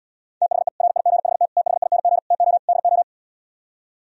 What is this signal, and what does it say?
Morse code, heard as a keyed tone, about 77 wpm, 700 Hz. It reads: THE RIGHT THING TO DO